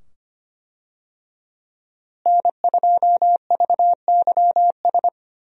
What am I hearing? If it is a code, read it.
Morse code, heard as a keyed tone, about 25 wpm, 700 Hz. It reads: N2VYS
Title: USA Call Signs - 5 Characters - Repeated 2x - 25wpm